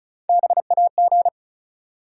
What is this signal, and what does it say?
Morse code, heard as a keyed tone, about 36 wpm, 700 Hz. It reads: BAG
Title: Three Letter Words - 35wpm